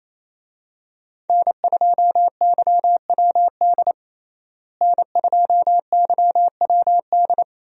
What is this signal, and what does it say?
Morse code, heard as a keyed tone, about 28 wpm, 700 Hz. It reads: N2YWD N2YWD
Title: USA Call Signs - 5 Characters - Repeated 3x - 28wpm